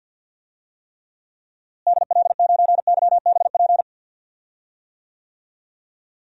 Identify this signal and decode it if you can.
Morse code, heard as a keyed tone, about 52 wpm, 700 Hz. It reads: DL9YBP
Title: DXCC-100 Call Signs - 6 Characters - Repeated 2x - 50wpm